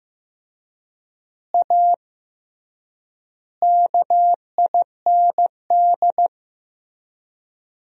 Morse code, heard as a keyed tone, about 15 wpm, 700 Hz. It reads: A KIND
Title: 3x Word-Spacing - Sentences from Top 500 Words - 15wpm